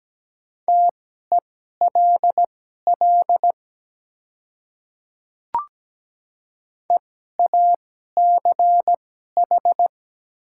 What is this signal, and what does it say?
Morse code, heard as a keyed tone, about 17 wpm, spaced as for 12 wpm, 700 Hz. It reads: TELL EACH
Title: 2x Character Spacing - Top 200 Words - 17wpm